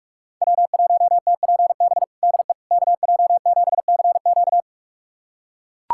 Morse code, heard as a keyed tone, about 43 wpm, 700 Hz. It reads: W1TPB DE KJ7CQ
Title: USA Callsign DE Callsign - 5 Characters - 45wpm